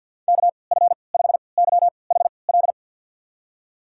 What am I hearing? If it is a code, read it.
Morse code, heard as a keyed tone, about 52 wpm, spaced as for 29 wpm, 690 Hz. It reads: KF5YHL
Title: USA Call Signs - 6 Characters - 3x Character Spacing - 50wpm